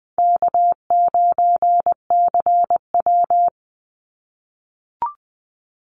Morse code, heard as a keyed tone, about 20 wpm, 700 Hz. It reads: K9CW
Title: Easy Call Signs - 20wpm